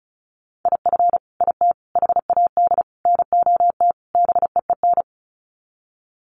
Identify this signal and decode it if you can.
Morse code, heard as a keyed tone, about 34 wpm, 700 Hz. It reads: IF IT HAD NOT BEEN
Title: Sets of 5 Words Encore 35wpm